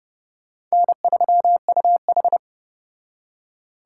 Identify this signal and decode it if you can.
Morse code, heard as a keyed tone, about 34 wpm, 700 Hz. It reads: N3UH